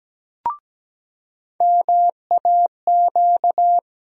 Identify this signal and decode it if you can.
Morse code, heard as a keyed tone, about 17 wpm, 700 Hz. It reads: MAQ
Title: USA Call Sign Postfixes - 3 Letters 17wpm